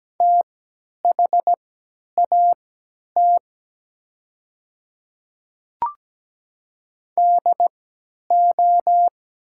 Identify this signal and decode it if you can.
Morse code, heard as a keyed tone, about 17 wpm, spaced as for 10 wpm, 700 Hz. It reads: THAT DO
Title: 3x Character Spacing - Top 100 Words - 17wpm